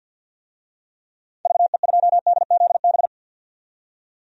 Morse code, heard as a keyed tone, about 52 wpm, 700 Hz. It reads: VE2DZB
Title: DXCC-100 Call Signs - 6 Characters - 50wpm